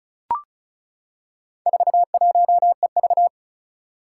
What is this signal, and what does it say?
Morse code, heard as a keyed tone, about 34 wpm, 700 Hz. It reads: 41EV
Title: Four Challenging Letter-Number Combinations 35wpm